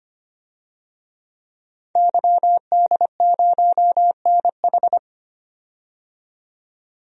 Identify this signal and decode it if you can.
Morse code, heard as a keyed tone, about 25 wpm, 700 Hz. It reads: YD0NH
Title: DXCC-100 Call Signs - 5 Characters - 25wpm